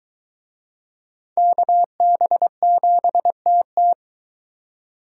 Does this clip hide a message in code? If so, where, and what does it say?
Yes: Morse code from 1.4 to 3.9 s, heard as a keyed tone, about 23 wpm, 700 Hz. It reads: KB7TT